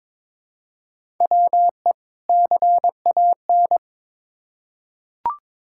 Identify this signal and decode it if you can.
Morse code, heard as a keyed tone, about 22 wpm, 700 Hz. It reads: WE CAN
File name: Build A Phrase - 2-Word Phrases - 22wpm